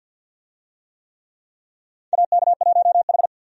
Morse code, heard as a keyed tone, about 50 wpm, 700 Hz. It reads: AK1H
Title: USA Call Signs - 4 Characters - 50wpm